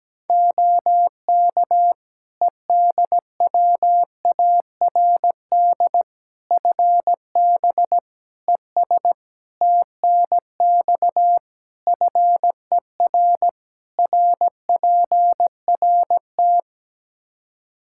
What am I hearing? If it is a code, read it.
Morse code, heard as a keyed tone, about 17 wpm, 700 Hz. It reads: OK EDWARD FB ES TNX FER RPRT
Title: Practice QSO - Exchange 2 - Info - 17wpm